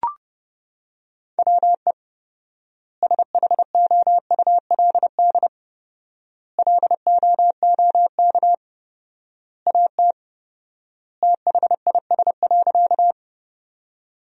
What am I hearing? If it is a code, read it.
Morse code, heard as a keyed tone, about 30 wpm, 700 Hz. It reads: WE SHOULD LOOK AT THIS.
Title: 4x Word-Spacing - Sentences from Top 300 Words - 30wpm